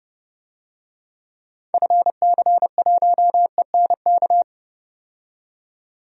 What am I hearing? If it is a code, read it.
Morse code, heard as a keyed tone, about 29 wpm, 700 Hz. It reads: FC1ENK
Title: DXCC-100 Call Signs - 6 Characters - 30wpm